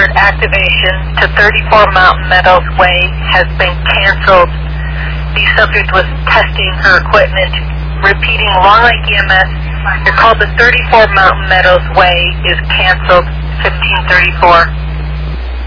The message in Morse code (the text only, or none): none